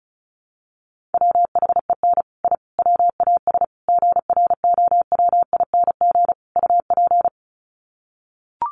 Morse code, heard as a keyed tone, about 34 wpm, 700 Hz. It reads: WHEN I WAS GROWING UP
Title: Sets of 5 Words Encore 35wpm